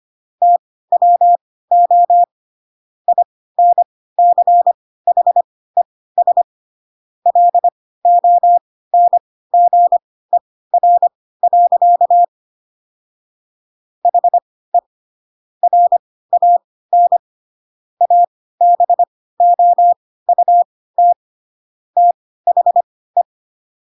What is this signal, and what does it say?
Morse code, heard as a keyed tone, about 25 wpm, spaced as for 16 wpm, 700 Hz. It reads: TWO INCHES LONGER. HE RAN ABOUT THE